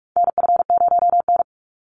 Morse code, heard as a keyed tone, about 44 wpm, 700 Hz. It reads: NF0N
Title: Call Signs Repeated 3x - 45wpm